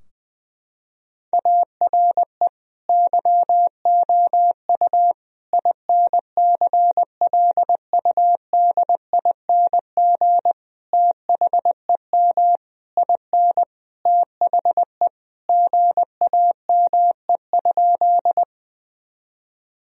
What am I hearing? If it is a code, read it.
Morse code, heard as a keyed tone, about 20 wpm, 700 Hz. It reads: ARE YOU INCLUDING THEM IN THE GAME?
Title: Sentences from Top 1000 Words - Review Part 4 - 20wpm